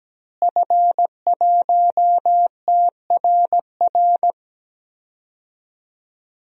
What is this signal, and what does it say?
Morse code, heard as a keyed tone, about 17 wpm, 700 Hz. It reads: F1TRR
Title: DXCC-100 Call Signs - 5 Characters - Repeated 3x - 17wpm